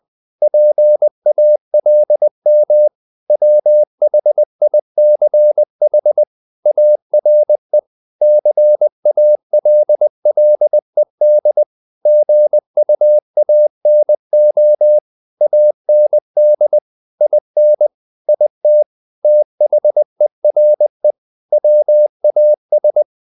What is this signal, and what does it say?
Morse code, heard as a keyed tone, about 20 wpm, 600 Hz. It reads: PALM WHICH ARE CALLED GUANO AND IN IT THERE WAS
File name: the old man and the sea / the old man and the sea part 1 20wpm